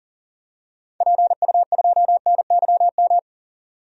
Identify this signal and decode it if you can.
Morse code, heard as a keyed tone, about 40 wpm, 700 Hz. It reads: PU2NYM
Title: DXCC-100 Call Signs - 6 Characters - 40wpm